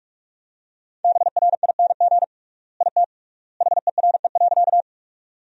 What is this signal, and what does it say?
Morse code, heard as a keyed tone, about 46 wpm, 700 Hz. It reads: BRING IT HERE.
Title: 3x Word-Spacing - Sentences from Top 500 Words - 45wpm